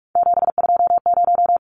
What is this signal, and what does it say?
Morse code, heard as a keyed tone, about 44 wpm, 700 Hz. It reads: 720
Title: Numbers - 3-Digit 45wpm